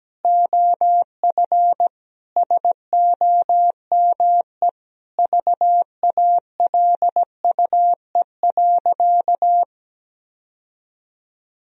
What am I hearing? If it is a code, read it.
Morse code, heard as a keyed tone, about 17 wpm, 700 Hz. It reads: OF SOME VALUE.
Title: Sentences from Top 1000 Words - Review Part 3 - 17wpm